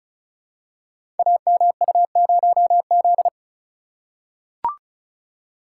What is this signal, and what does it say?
Morse code, heard as a keyed tone, about 36 wpm, 700 Hz. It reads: AMU0Z
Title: Code Groups - 5 Letter:Numbers - 35wpm